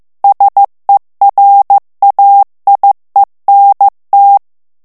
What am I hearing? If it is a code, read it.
Morse code, heard as a keyed tone, about 15 wpm, 800 Hz. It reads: SERAIENT
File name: FRANCAIS TEXTE CLAIR 2 VITESSE 14 MOTS MINUTES